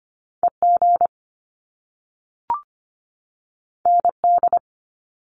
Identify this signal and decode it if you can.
Morse code, heard as a keyed tone, about 25 wpm, 700 Hz. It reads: EG ND
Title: Top Two Letter Combinations in Words 25wpm